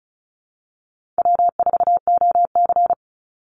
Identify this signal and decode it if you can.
Morse code, heard as a keyed tone, about 34 wpm, 700 Hz. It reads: W4OC
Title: Easy Call Signs Repeated 2x - 35wpm